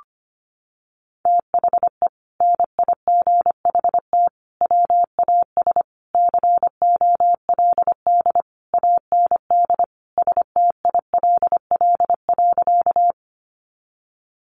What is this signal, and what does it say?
Morse code, heard as a keyed tone, about 25 wpm, 700 Hz. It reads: THE NIGHT WAS COLD AND STILL.
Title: Sentences from Top 400 Words 25wpm